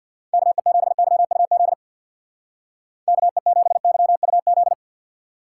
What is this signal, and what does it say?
Morse code, heard as a keyed tone, about 61 wpm, 700 Hz. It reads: KE7YUZ KE7YUZ
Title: Speed Racing - USA 2x3 Call Signs 50wpm